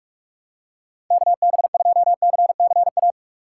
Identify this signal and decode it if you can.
Morse code, heard as a keyed tone, about 45 wpm, 700 Hz. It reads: KB2CCA
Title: USA Call Signs - 6 Characters - 4x Character Spacing - 45wpm